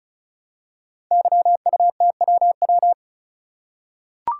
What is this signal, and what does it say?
Morse code, heard as a keyed tone, about 35 wpm, 700 Hz. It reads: YUTWW
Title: Code Groups - 5 Letters - 35wpm